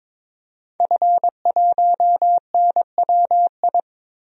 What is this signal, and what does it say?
Morse code, heard as a keyed tone, about 22 wpm, 700 Hz. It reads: F1NWI
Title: DXCC-100 Call Signs - 5 Characters - 22wpm